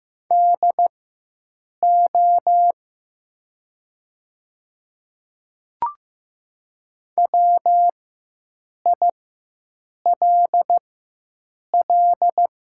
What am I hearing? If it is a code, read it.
Morse code, heard as a keyed tone, about 15 wpm, spaced as for 7 wpm, 700 Hz. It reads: DO WILL
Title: 4x Character Spacing - Top 200 Words - 15wpm